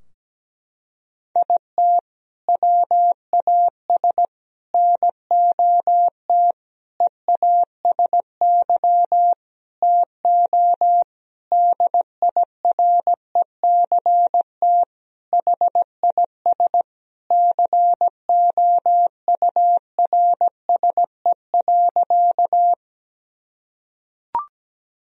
Morse code, heard as a keyed tone, about 17 wpm, 700 Hz. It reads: IT WAS NOT EASY TO DIRECT HIS COURSE.
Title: Sentences from Top 1000 Words - Review Part 2 - 17wpm